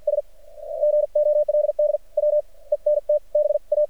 Morse code, heard as a keyed tone, about 53 wpm, spaced as for 39 wpm, 600 Hz. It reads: S GOLD WENT BA
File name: archived / CRSnet